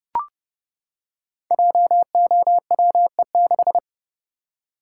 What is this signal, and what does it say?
Morse code, heard as a keyed tone, about 31 wpm, 700 Hz. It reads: JOWE6
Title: Code Groups - 5 Letter:Numbers - 30wpm